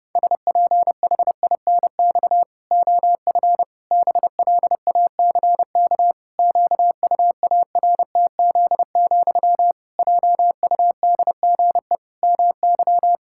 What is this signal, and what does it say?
Morse code, heard as a keyed tone, about 32 wpm, 700 Hz. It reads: SPHINX OF BLACK QUARTZ, JUDGE MY
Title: Pangram Challenge v2 - 30wpm